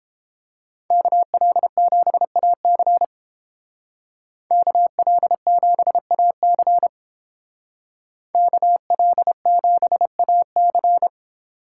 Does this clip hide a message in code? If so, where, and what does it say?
Yes: Morse code from 0.9 to 11.1 s, heard as a keyed tone, about 30 wpm, 700 Hz. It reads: KL7AC KL7AC KL7AC